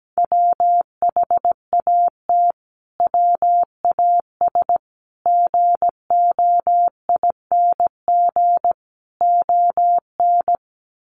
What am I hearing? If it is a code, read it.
Morse code, heard as a keyed tone, about 17 wpm, 700 Hz. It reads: WHAT WAS GOING ON
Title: Sets of 4 Words 17wpm